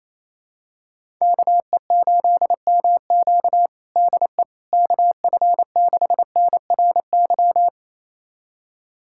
Morse code, heard as a keyed tone, about 28 wpm, 700 Hz. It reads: KE8MQ DE KF6NRY
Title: USA Callsign DE Callsign 28wpm